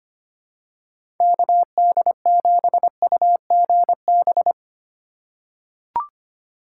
Morse code, heard as a keyed tone, about 25 wpm, 700 Hz. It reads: KD7UGB
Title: USA Call Signs - 6 Characters - 4x Character Spacing - 25wpm